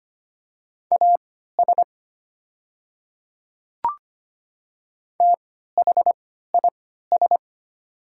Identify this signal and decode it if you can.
Morse code, heard as a keyed tone, about 25 wpm, spaced as for 14 wpm, 700 Hz. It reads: AS THIS
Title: 3x Character Spacing - Top 100 Words - 25wpm